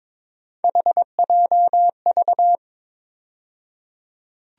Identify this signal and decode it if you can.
Morse code, heard as a keyed tone, about 22 wpm, 700 Hz. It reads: HJV
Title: USA Call Sign Postfixes - 3 Letters 22wpm